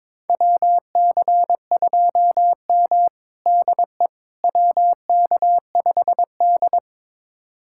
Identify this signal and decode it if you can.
Morse code, heard as a keyed tone, about 22 wpm, 700 Hz. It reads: WC2M DE WK5D
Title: USA Callsign DE Callsign - 4 Characters - 22wpm